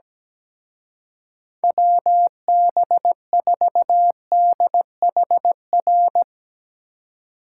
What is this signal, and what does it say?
Morse code, heard as a keyed tone, about 18 wpm, 700 Hz. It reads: WB4DHR
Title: Speed Racing - USA 2x3 Call Signs 17wpm